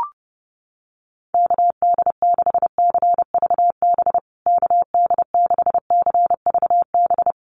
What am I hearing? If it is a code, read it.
Morse code, heard as a keyed tone, about 30 wpm, 700 Hz. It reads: KD6CVB KD6CVB
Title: USA Call Signs Repeated 3x 30wpm